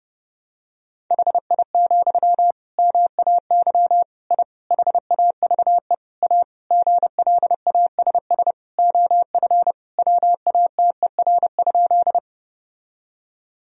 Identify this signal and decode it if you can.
Morse code, heard as a keyed tone, about 30 wpm, 700 Hz. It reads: HI, MAY I HAVE A GLASS OF WATER?
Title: Sentences from Top 1000 Words - Review Part 3 - 30wpm